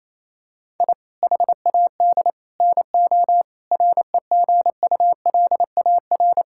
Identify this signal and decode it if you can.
Morse code, heard as a keyed tone, about 27 wpm, 700 Hz. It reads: I HAD NO REGULAR
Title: Sentences from Top 1000 Words - Review Part 6 - 28wpm